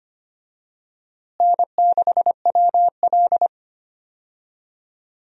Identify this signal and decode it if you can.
Morse code, heard as a keyed tone, about 25 wpm, 700 Hz. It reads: N6WL
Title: USA Call Signs - 4 Characters - Repeated 3x - 25wpm